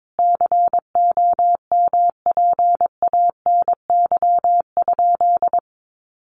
Morse code, heard as a keyed tone, about 22 wpm, 700 Hz. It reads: COMPANY?
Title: Sentences from Top 600 Words 22wpm